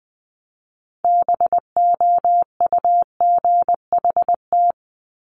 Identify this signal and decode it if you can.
Morse code, heard as a keyed tone, about 20 wpm, 700 Hz. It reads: BOUGHT
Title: Multi-letter Phonemes Repeated 3x 20wpm